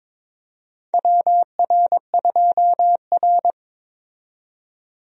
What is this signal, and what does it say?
Morse code, heard as a keyed tone, about 22 wpm, 700 Hz. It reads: WR2R
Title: USA Call Signs - 4 Characters - 22wpm